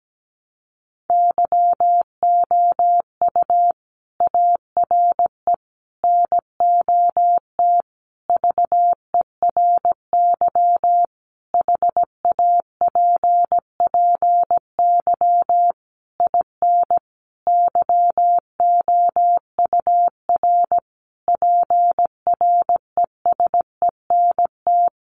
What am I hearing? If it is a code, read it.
Morse code, heard as a keyed tone, about 17 wpm, 700 Hz. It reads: YOU ARE NOT VERY HAPPY IN YOUR PRESENT
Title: Sentences from Top 1200 Words 17wpm